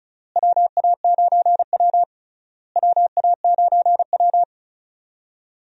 Morse code, heard as a keyed tone, about 36 wpm, 700 Hz. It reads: WA9W WA9W
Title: USA Call Signs - 4 Characters - Repeated 3x - 35wpm